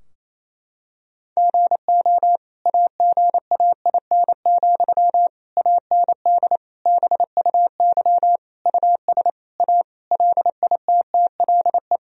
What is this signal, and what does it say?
Morse code, heard as a keyed tone, about 27 wpm, 700 Hz. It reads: GO AGAIN, AND BUY US A LITTLE
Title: Sentences from Top 1000 Words - Review Part 4 - 28wpm